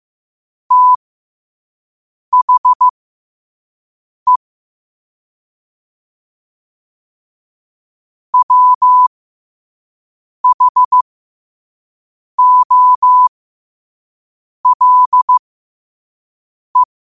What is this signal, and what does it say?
Morse code, heard as a keyed tone, about 15 wpm, spaced as for 5 wpm, 1000 Hz. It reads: THE WHOLE
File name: 5wpm